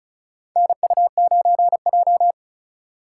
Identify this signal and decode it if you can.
Morse code, heard as a keyed tone, about 34 wpm, 690 Hz. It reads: NU9J
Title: USA Call Signs - 4 Characters - 2x Character Spacing - 35wpm